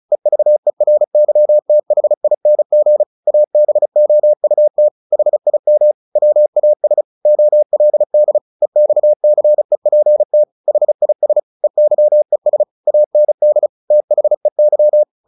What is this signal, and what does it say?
Morse code, heard as a keyed tone, about 34 wpm, 600 Hz. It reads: EVERYTHING ABOUT HIM WAS OLD EXCEPT HIS EYES AND THEY
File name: the old man and the sea / the old man and the sea 1 35wpm